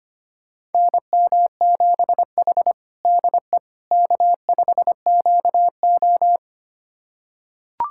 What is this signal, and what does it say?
Morse code, heard as a keyed tone, about 25 wpm, 700 Hz. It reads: NM7H DE K5QO
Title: USA Callsign DE Callsign - 4 Characters - 25wpm